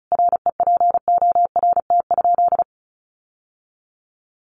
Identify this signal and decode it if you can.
Morse code, heard as a keyed tone, about 34 wpm, 700 Hz. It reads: REPORT?
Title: Sentences from Top 1200 Words 35wpm